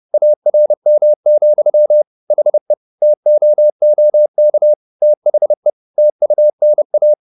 Morse code, heard as a keyed tone, about 30 wpm, 600 Hz. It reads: ARM, HE TOOK THE TUNA
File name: the old man and the sea / the old man and the sea 1 30wpm